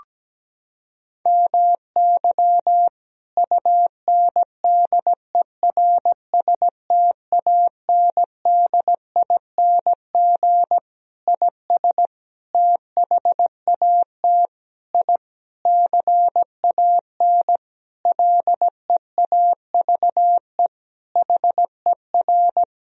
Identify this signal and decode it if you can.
Morse code, heard as a keyed tone, about 17 wpm, 700 Hz. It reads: MY UNDERSTANDING IS THAT I CAN LEAVE HER
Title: Sentences from Top 1000 Words - Review Part 3 - 17wpm